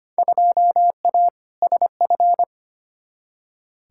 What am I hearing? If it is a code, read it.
Morse code, heard as a keyed tone, about 25 wpm, 700 Hz. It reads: 2A SF
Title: Field Day Exchanges - 25wpm